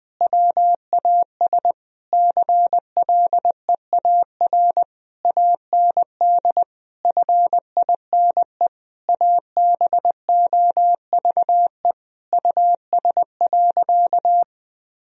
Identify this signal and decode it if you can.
Morse code, heard as a keyed tone, about 21 wpm, 700 Hz. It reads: WAS CLEAR AND FINE ABOVE US.
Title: Sentences from Top 1000 Words - Review Part 2 - 20wpm